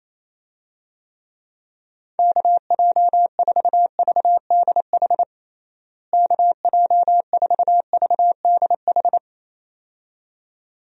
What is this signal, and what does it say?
Morse code, heard as a keyed tone, about 28 wpm, 700 Hz. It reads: KJ4VDH KJ4VDH
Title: USA Call Signs - 6 Characters - Repeated 2x - 28wpm